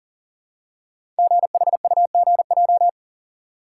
Morse code, heard as a keyed tone, about 40 wpm, 700 Hz. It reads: GHUGJ
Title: Code Groups - 5 Letters - 40wpm